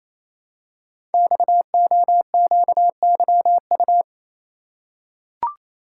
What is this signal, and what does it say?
Morse code, heard as a keyed tone, about 28 wpm, 700 Hz. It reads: XOQYU
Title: Code Groups - 5 Letters - 28wpm